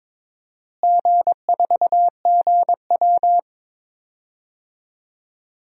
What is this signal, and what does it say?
Morse code, heard as a keyed tone, about 22 wpm, 700 Hz. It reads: G4GW
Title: DXCC-100 Call Signs - 4 Characters - Repeated 3x - 22wpm